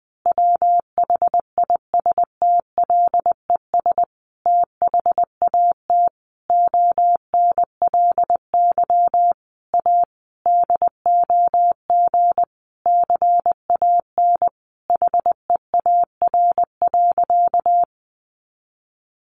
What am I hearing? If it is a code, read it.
Morse code, heard as a keyed tone, about 20 wpm, 700 Hz. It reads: WHISTLES THAT ONLY A DOG CAN HEAR.
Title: A Sound Of Thunder 20wpm